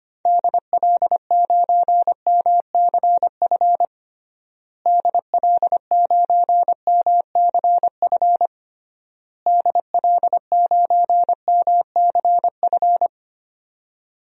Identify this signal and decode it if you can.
Morse code, heard as a keyed tone, about 25 wpm, 700 Hz. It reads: DL9MCF DL9MCF DL9MCF
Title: DXCC-100 Call Signs - 6 Characters - Repeated 3x - 25wpm